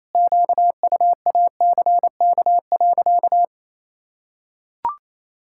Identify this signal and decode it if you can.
Morse code, heard as a keyed tone, about 28 wpm, 700 Hz. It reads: QUACK.
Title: Pangram Challenge v2 - 28wpm